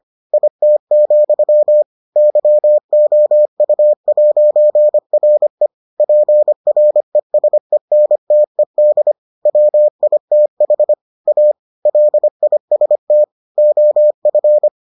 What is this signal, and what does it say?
Morse code, heard as a keyed tone, about 25 wpm, 600 Hz. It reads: IT, YOU'RE PRESENTED WITH A LIST OF